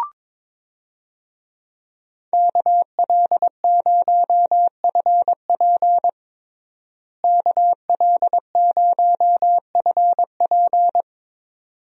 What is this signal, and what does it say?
Morse code, heard as a keyed tone, about 22 wpm, 700 Hz. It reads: KL0FP KL0FP
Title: USA Call Signs - 5 Characters - Repeated 2x - 22wpm